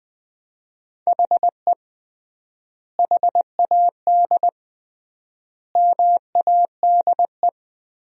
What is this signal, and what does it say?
Morse code, heard as a keyed tone, about 21 wpm, 700 Hz. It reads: HE HAD MADE
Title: 3x Word-Spacing - Sentences from Top 500 Words - 20wpm